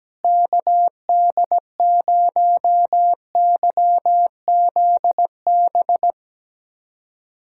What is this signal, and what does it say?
Morse code, heard as a keyed tone, about 17 wpm, 700 Hz. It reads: KD0YZB
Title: USA Callsign DE Callsign - 6 Characters - 17wpm